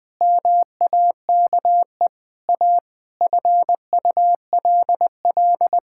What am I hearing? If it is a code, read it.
Morse code, heard as a keyed tone, about 20 wpm, 700 Hz. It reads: MAKE A FULL
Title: Sentences with 7-Word Phrases - 20wpm